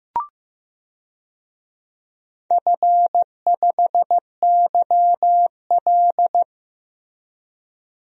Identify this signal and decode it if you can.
Morse code, heard as a keyed tone, about 15 wpm, 700 Hz. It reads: F5YL
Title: DXCC-100 Call Signs - 4 Characters - Repeated 2x - 15wpm